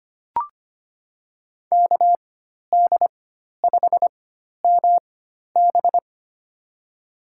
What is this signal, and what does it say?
Morse code, heard as a keyed tone, about 25 wpm, spaced as for 12 wpm, 700 Hz. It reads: KD5MB
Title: USA Call Signs - 4x Character Spacing - 25wpm